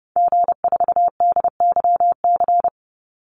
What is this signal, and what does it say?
Morse code, heard as a keyed tone, about 30 wpm, 700 Hz. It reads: G4DYC
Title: Call Signs Repeated 2x - 30wpm